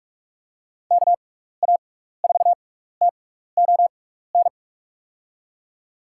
Morse code, heard as a keyed tone, about 46 wpm, spaced as for 16 wpm, 700 Hz. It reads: KA4TON
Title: USA Call Signs - 6 Characters - 6x Character Spacing - 45wpm